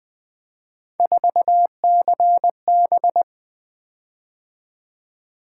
Morse code, heard as a keyed tone, about 20 wpm, 700 Hz. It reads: 4CB
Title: Three Letter-Number Combinations - 20wpm